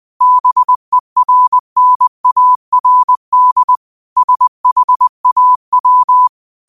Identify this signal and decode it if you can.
Morse code, heard as a keyed tone, about 20 wpm, 1000 Hz. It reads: BERNARD SHAW
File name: qotdfiles / 20wpm